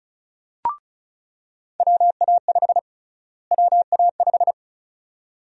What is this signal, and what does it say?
Morse code, heard as a keyed tone, about 34 wpm, 700 Hz. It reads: WA5 WA5
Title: USA Call Sign Prefixes - Repeated 2x - 35wpm